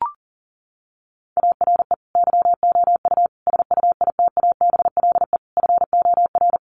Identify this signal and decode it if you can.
Morse code, heard as a keyed tone, about 40 wpm, 700 Hz. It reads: ARE YOU SUITABLE FOR
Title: Sentences from Top 1400 Words 40wpm